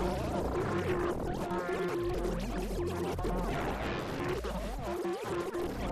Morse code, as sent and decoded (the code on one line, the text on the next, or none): none